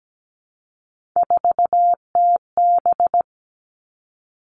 Morse code, heard as a keyed tone, about 17 wpm, 700 Hz. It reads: 4TB
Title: All Cut Numbers - 3-Digit 17wpm